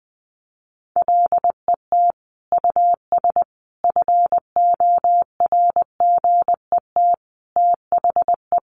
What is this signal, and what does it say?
Morse code, heard as a keyed tone, about 20 wpm, 700 Hz. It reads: LET US FORGET THE
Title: Sentences from Top 1200 Words 20wpm